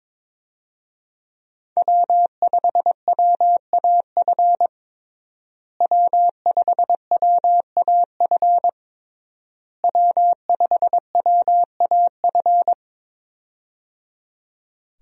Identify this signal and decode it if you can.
Morse code, heard as a keyed tone, about 22 wpm, 700 Hz. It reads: W5WAF W5WAF W5WAF
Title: USA Call Signs - 5 Characters - Repeated 3x - 22wpm